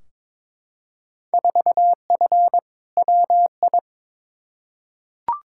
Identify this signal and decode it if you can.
Morse code, heard as a keyed tone, about 21 wpm, 700 Hz. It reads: 4F WI
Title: Field Day Exchanges Encore - 22wpm